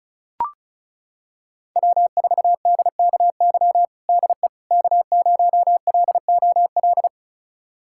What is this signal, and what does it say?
Morse code, heard as a keyed tone, about 36 wpm, 690 Hz. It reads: W4DKY DE K0LOL
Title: USA Callsign DE Callsign - 5 Characters - 35wpm